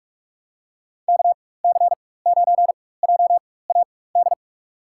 Morse code, heard as a keyed tone, about 43 wpm, spaced as for 21 wpm, 690 Hz. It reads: KC9JAD
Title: USA Call Signs - 4x Character Spacing - 45wpm